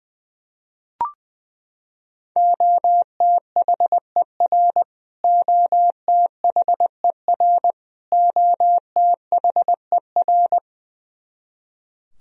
Morse code, heard as a keyed tone, about 20 wpm, 700 Hz. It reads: OTHER OTHER OTHER
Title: Top 100 Words Repeated 20wpm